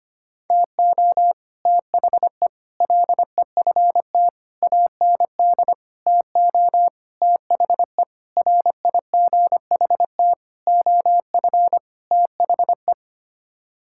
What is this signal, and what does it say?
Morse code, heard as a keyed tone, about 25 wpm, 700 Hz. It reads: TO THE LEFT AND TO THE RIGHT OF THE
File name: Sets of 9 Words - 25wpm